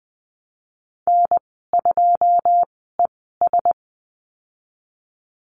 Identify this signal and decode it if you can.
Morse code, heard as a keyed tone, about 20 wpm, spaced as for 15 wpm, 700 Hz. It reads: N2ES
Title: 186 Lesson 30g - Letter b - Call Signs 20wpm